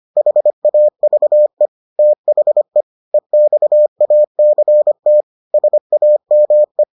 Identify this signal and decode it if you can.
Morse code, heard as a keyed tone, about 25 wpm, 600 Hz. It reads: HAVE THE EXACT SAME